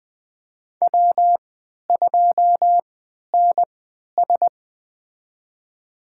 Morse code, heard as a keyed tone, about 21 wpm, spaced as for 11 wpm, 700 Hz. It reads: W2NS
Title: USA Call Signs - 3x Character Spacing - 20wpm